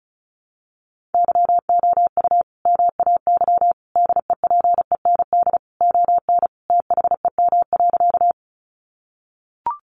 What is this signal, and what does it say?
Morse code, heard as a keyed tone, about 34 wpm, 700 Hz. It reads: YOU MAY DEPEND ON THEM.
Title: Sentences from Top 1300 Words 35wpm